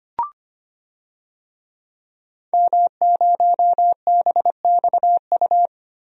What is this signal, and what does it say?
Morse code, heard as a keyed tone, about 25 wpm, 700 Hz. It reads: M0BXU
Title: DXCC-100 Call Signs - 5 Characters - Repeated 3x - 25wpm